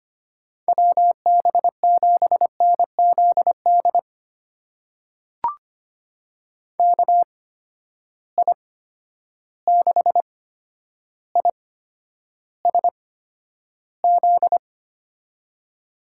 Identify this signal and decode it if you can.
Morse code, heard as a keyed tone, about 25 wpm, spaced as for 11 wpm, 700 Hz. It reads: WB7NZD KI6ISZ
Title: USA Call Signs - 6 Characters - 8x Character Spacing - 25wpm